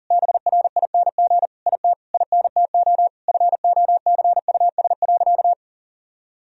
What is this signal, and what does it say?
Morse code, heard as a keyed tone, about 40 wpm, 700 Hz. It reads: BRING IT INTO FOCUS.
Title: Sentences from Top 1000 Words - Review Part 4 - 40wpm